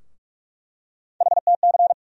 Morse code, heard as a keyed tone, about 46 wpm, 700 Hz. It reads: HTC
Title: USA Call Sign Postfixes - 3 Letters 45wpm